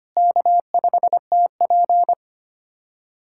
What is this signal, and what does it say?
Morse code, heard as a keyed tone, about 25 wpm, 700 Hz. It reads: K5TP
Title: USA Callsign DE Callsign - 4 Characters - 25wpm